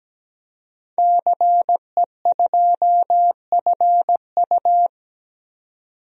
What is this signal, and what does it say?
Morse code, heard as a keyed tone, about 17 wpm, 700 Hz. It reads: CE2FU
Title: DXCC-100 Call Signs - 5 Characters - 17wpm